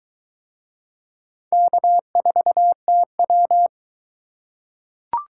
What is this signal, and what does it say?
Morse code, heard as a keyed tone, about 20 wpm, 700 Hz. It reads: K4TW